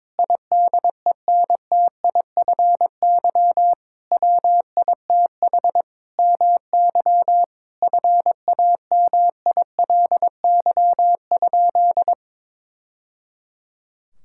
Morse code, heard as a keyed tone, about 22 wpm, 700 Hz. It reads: IDENTIFY WITH MY FAMILY?
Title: Sentences from Top 1000 Words - Review Part 6 - 22wpm